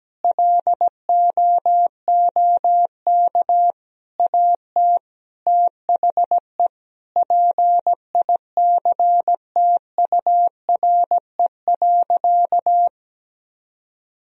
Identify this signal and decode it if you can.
Morse code, heard as a keyed tone, about 17 wpm, 700 Hz. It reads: LOOK AT THE PICTURE.
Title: Sentences from Top 1000 Words - Review Part 1 - 17wpm